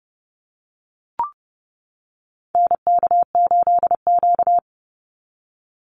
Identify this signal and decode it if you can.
Morse code, heard as a keyed tone, about 30 wpm, 700 Hz. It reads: NK8Q
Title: NA SOTA Activator Callsigns 30wpm